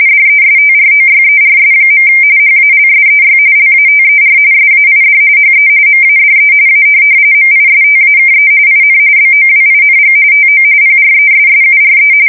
RTTY, about 45.45 baud. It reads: DESTROYERS. HALSEY LATER SAID HE MADE A MISTAKE IN BOWING TO PRESSURE
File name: fsk170 / USS-New-Jersey